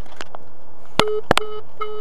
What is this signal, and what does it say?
There is no code.